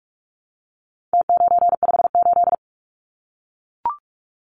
Morse code, heard as a keyed tone, about 44 wpm, 700 Hz. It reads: T958
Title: Numbers - Contest Style - 4-Digit 45wpm